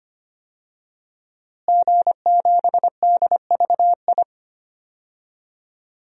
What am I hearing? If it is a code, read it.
Morse code, heard as a keyed tone, about 25 wpm, 700 Hz. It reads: G7DVI
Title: DXCC-100 Call Signs - 5 Characters - Repeated 3x - 25wpm